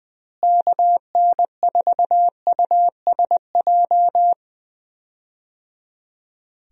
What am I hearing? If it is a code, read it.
Morse code, heard as a keyed tone, about 21 wpm, 700 Hz. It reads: KN4USJ
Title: USA Call Signs - 6 Characters - Repeated 3x - 20wpm